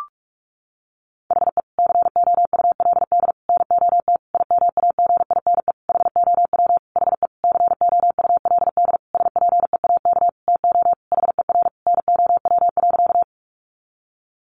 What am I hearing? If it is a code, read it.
Morse code, heard as a keyed tone, about 44 wpm, 700 Hz. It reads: HE COULD NOT IMAGINE HOW HE COULD SPEAK TO HER NOW.
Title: Sentences from Top 1400 Words 45wpm